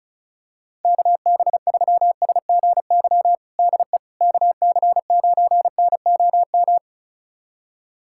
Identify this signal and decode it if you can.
Morse code, heard as a keyed tone, about 36 wpm, 700 Hz. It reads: KB3SGY DE KC9NOM
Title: USA Callsign DE Callsign - 6 Characters - 35wpm